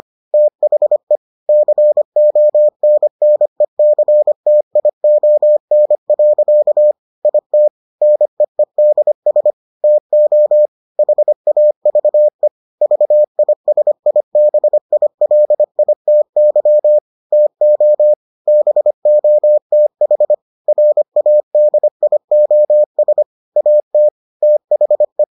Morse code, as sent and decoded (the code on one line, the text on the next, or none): - .... / -.-. --- -. -. -.-. - .. --- -. .-.-.- / .. - / -. . -.. ... / - --- / .... .- ...- . / ...- .. ... .. -... .. .-.. .. - -.-- / - --- / -... --- - .... / .-. .- -.. .. --- ... / .- - / - .... .
THE CONNECTION. IT NEEDS TO HAVE VISIBILITY TO BOTH RADIOS AT THE